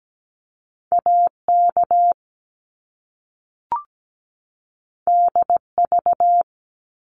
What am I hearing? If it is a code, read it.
Morse code, heard as a keyed tone, about 17 wpm, 700 Hz. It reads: AK DV
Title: Top Two Letter Combinations in Words 17wpm